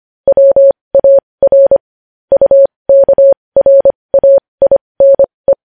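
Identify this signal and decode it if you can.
Morse code, heard as a keyed tone, about 25 wpm, spaced as for 20 wpm, 550 Hz. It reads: WAR UKRAINE
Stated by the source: Morse code sending news headlines at 20 wpm.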